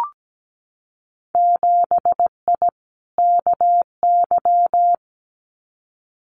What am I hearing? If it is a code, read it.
Morse code, heard as a keyed tone, about 17 wpm, 700 Hz. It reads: 7I KY
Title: Winter Field Day Exchanges Encore 17wpm